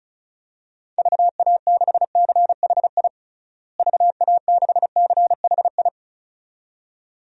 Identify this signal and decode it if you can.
Morse code, heard as a keyed tone, about 34 wpm, 700 Hz. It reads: VA6CHI VA6CHI
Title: DXCC-100 Call Signs - 6 Characters - Repeated 2x - 35wpm